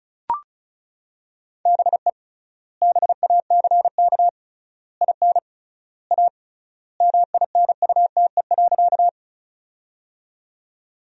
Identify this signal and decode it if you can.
Morse code, heard as a keyed tone, about 34 wpm, 700 Hz. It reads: BE BACK IN A MINUTE.
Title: 3x Word-Spacing - Sentences from Top 500 Words - 35wpm